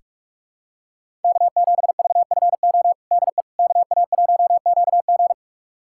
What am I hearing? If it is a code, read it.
Morse code, heard as a keyed tone, about 47 wpm, 700 Hz. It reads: K7VFO DE KA1QG
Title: USA Callsign DE Callsign - 5 Characters - 45wpm